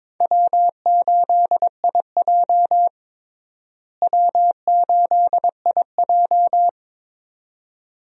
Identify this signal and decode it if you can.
Morse code, heard as a keyed tone, about 22 wpm, 700 Hz. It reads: W8IJ W8IJ
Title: USA Call Signs - 4 Characters - Repeated 2x - 22wpm